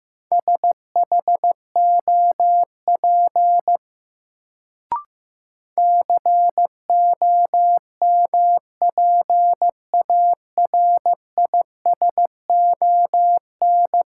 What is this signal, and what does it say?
Morse code, heard as a keyed tone, about 16 wpm, 700 Hz. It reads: SHOP COMPARISON
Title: Sentences from Top 1600 Words - New Word Drill Alt - 15wpm